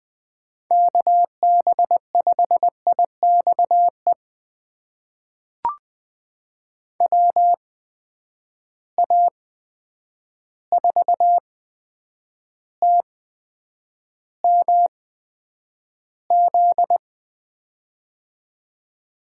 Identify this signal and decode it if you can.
Morse code, heard as a keyed tone, about 20 wpm, spaced as for 9 wpm, 700 Hz. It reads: KB5IXE WA4TMZ
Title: USA Call Signs - 6 Characters - 8x Character Spacing - 20wpm